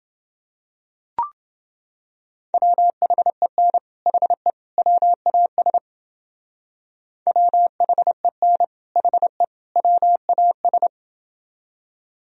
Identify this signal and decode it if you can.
Morse code, heard as a keyed tone, about 29 wpm, 700 Hz. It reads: WHEN HE WAS WHEN HE WAS